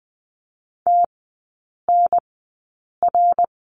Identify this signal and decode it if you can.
Morse code, heard as a keyed tone, about 20 wpm, spaced as for 8 wpm, 700 Hz. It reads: TNR
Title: 053 Lesson 10e - Letter r - Three Characters 20wpm